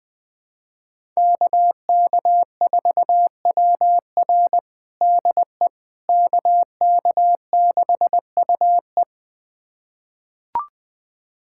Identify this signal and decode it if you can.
Morse code, heard as a keyed tone, about 20 wpm, 700 Hz. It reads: KK4WR DE KK6UE
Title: USA Callsign DE Callsign - 5 Characters - 20wpm